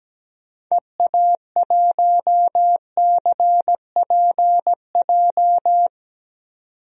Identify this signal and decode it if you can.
Morse code, heard as a keyed tone, about 17 wpm, 700 Hz. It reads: EA1CPJ
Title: DXCC-100 Call Signs - 6 Characters - 17wpm